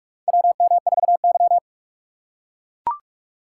Morse code, heard as a keyed tone, about 42 wpm, 700 Hz. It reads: WM4Y
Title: USA Call Signs - 4 Characters - 4x Character Spacing - 45wpm